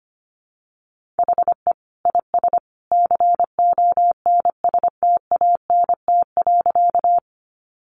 Morse code, heard as a keyed tone, about 25 wpm, 700 Hz. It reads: HE IS CONSTANT.
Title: Sentences from Top 1500 Words 25wpm